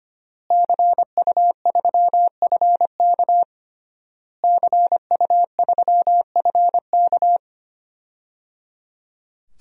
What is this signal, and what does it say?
Morse code, heard as a keyed tone, about 25 wpm, 700 Hz. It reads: CU3FK CU3FK
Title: DXCC-100 Call Signs - 5 Characters - Repeated 3x - 25wpm